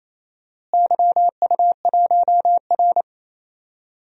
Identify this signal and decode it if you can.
Morse code, heard as a keyed tone, about 28 wpm, 700 Hz. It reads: YU1R
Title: DXCC-100 Call Signs - 4 Characters - Repeated 2x - 28wpm